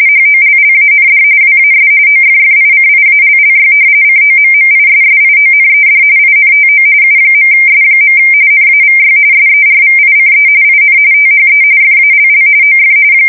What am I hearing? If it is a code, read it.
RTTY, about 45.45 baud. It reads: HOLLAND WAS OFFICIALLY COMMISSIONED INTO U.S. NAVY SERVICE IN THE FALL OF
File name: fsk170 / usnavy3